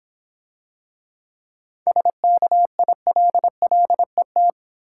Morse code, heard as a keyed tone, about 26 wpm, 690 Hz. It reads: SKILLET